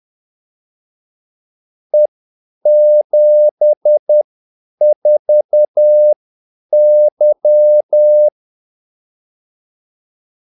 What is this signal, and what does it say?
Morse code, heard as a keyed tone, about 10 wpm, spaced as for 8 wpm, 600 Hz. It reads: E74Y
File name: Callsigns-10-8wpm